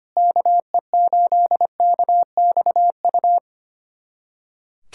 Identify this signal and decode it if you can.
Morse code, heard as a keyed tone, about 25 wpm, 700 Hz. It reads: KE8KXU
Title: USA Callsign DE Callsign 25wpm